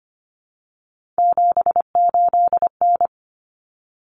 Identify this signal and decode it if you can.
Morse code, heard as a keyed tone, about 25 wpm, 700 Hz. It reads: 78N
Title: Common Cut Numbers - 3-Digit 25wpm